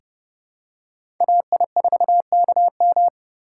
Morse code, heard as a keyed tone, about 30 wpm, 700 Hz. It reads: AI4KM
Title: USA Call Signs - 5 Characters - 2x Character Spacing - 30wpm